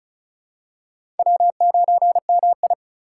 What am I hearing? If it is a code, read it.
Morse code, heard as a keyed tone, about 36 wpm, 700 Hz. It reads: W9MI
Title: USA Call Signs - 2x Character Spacing - 35wpm